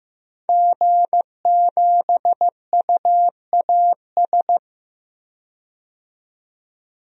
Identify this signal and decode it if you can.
Morse code, heard as a keyed tone, about 15 wpm, 700 Hz. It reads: G7UAS
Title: DXCC-100 Call Signs - 5 Characters - 15wpm